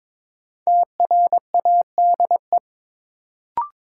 Morse code, heard as a keyed tone, about 22 wpm, 700 Hz. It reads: TRADE
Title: Sets of 5 Words Encore - New Word Drill - 22wpm